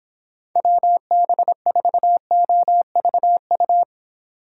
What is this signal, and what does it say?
Morse code, heard as a keyed tone, about 25 wpm, 700 Hz. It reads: WB4OVU